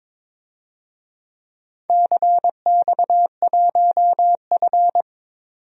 Morse code, heard as a keyed tone, about 22 wpm, 700 Hz. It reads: CX1F
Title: DXCC-100 Call Signs - 4 Characters - 22wpm